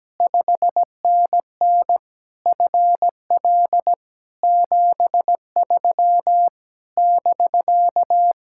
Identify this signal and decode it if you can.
Morse code, heard as a keyed tone, about 17 wpm, 700 Hz. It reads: 5NN FL 73 <BK>
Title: Speed-Racing Common QSO Phrases Encore - 17wpm